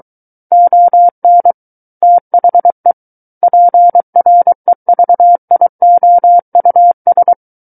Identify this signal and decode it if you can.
Morse code, heard as a keyed tone, about 23 wpm, 700 Hz. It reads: ON THE PREVIOUS
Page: SKCC Learning Center - Morse Code Practice Files